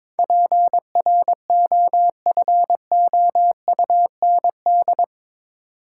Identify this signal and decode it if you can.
Morse code, heard as a keyed tone, about 22 wpm, 700 Hz. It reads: PROFOUND
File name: practice / 22wpm / Eight Letter Words - 22wpm